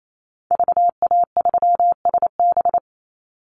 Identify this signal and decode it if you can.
Morse code, heard as a keyed tone, about 28 wpm, 700 Hz. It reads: VA3SB
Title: Call Signs Repeated 2x - 28wpm